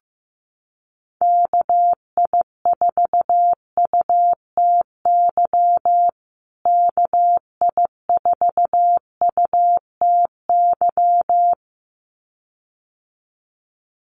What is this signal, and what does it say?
Morse code, heard as a keyed tone, about 15 wpm, 700 Hz. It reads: KI4UTY KI4UTY
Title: USA Call Signs Repeated 2x 15wpm